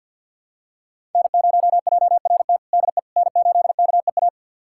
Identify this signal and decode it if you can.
Morse code, heard as a keyed tone, about 52 wpm, 700 Hz. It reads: N0JRT DE N8KEA
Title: USA Callsign DE Callsign - 5 Characters - 50wpm